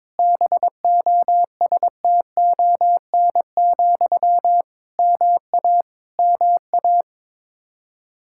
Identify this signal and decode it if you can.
Morse code, heard as a keyed tone, about 22 wpm, 700 Hz. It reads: BOSTON, MA MA
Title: Practice QSO - Exchange 1 - Beginning - 22wpm